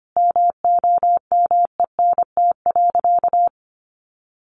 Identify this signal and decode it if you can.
Morse code, heard as a keyed tone, about 25 wpm, 700 Hz. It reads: MOMENT.
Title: Sentences from Top 1500 Words 25wpm